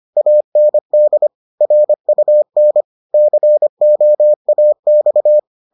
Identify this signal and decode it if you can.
Morse code, heard as a keyed tone, about 25 wpm, 600 Hz. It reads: AND RUN COAX